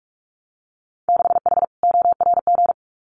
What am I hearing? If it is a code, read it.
Morse code, heard as a keyed tone, about 44 wpm, 700 Hz. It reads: 6H ORG
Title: Winter Field Day Exchanges Encore 45wpm